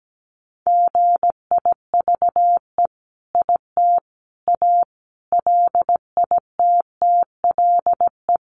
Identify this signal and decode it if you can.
Morse code, heard as a keyed tone, about 17 wpm, 700 Hz. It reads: GIVE IT A LITTLE
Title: Sentences from Top 1500 Words 17wpm